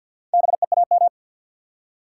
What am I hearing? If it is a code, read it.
Morse code, heard as a keyed tone, about 50 wpm, 700 Hz. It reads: BEAM
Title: Top 90 QSO Elements 50wpm